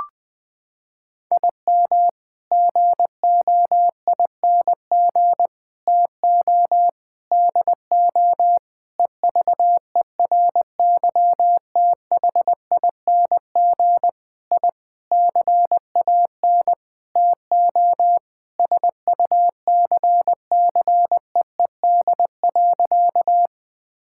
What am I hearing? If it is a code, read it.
Morse code, heard as a keyed tone, about 20 wpm, 700 Hz. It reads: IM GOING TO DO EVERYTHING I CAN TO SUCCEED.
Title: Sentences with 8-Word Phrases - 20wpm